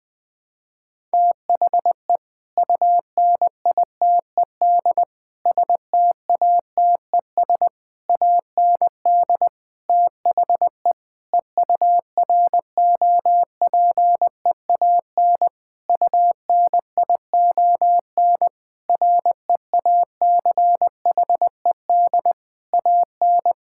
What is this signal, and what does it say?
Morse code, heard as a keyed tone, about 21 wpm, 700 Hz. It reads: THE UNITED STATES AND THE EUROPEAN UNION REACHED AN
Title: Sentences with 5-Word Phrases - 20wpm